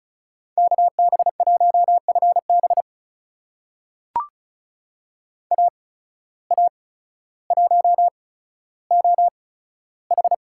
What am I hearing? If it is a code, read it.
Morse code, heard as a keyed tone, about 36 wpm, spaced as for 15 wpm, 700 Hz. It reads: KB1FB AA1OH
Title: USA Call Signs - 5 Characters - 8x Character Spacing - 35wpm